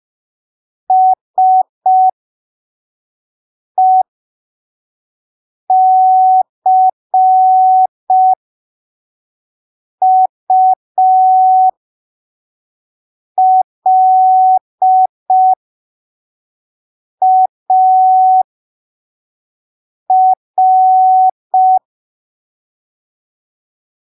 Morse code, heard as a keyed tone, about 5 wpm, spaced as for 3 wpm, 750 Hz. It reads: SECULAR